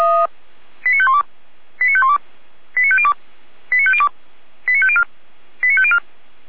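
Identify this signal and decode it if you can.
Five-tone selective call (ZVEI), heard as quick runs of five tones, tones of about 70 ms. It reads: 89512 89512 89502 89502 89504 89504
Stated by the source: ZVEI five-tone selective call received by SDR.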